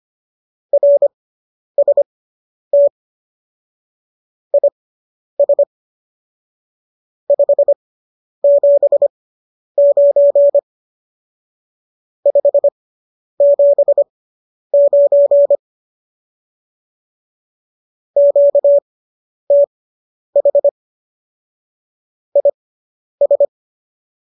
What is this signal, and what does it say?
Morse code, heard as a keyed tone, about 25 wpm, spaced as for 10 wpm, 570 Hz. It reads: RST IS 579 579 QTH IS